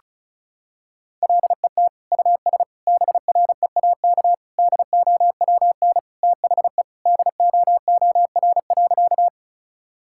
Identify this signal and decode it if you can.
Morse code, heard as a keyed tone, about 34 wpm, 700 Hz. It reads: LET US BREAK DOWN THE DOOR.
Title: Sentences from Top 1000 Words - Review Part 6 - 35wpm